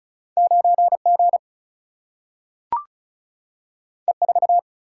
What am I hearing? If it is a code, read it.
Morse code, heard as a keyed tone, about 36 wpm, 700 Hz. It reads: 9G E4
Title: Single-Letter Single-Number 35wpm